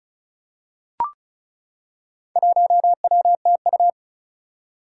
Morse code, heard as a keyed tone, about 36 wpm, 700 Hz. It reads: 1WTU
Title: Four Challenging Letter-Number Combinations 35wpm